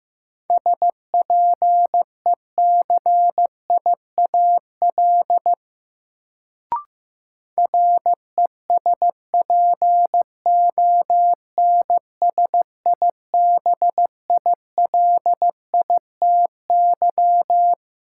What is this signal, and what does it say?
Morse code, heard as a keyed tone, about 15 wpm, 700 Hz. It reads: SPECIAL RESPONSIBILITY
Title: Sentences from Top 1200 Words - New Word Drill Alt - 15wpm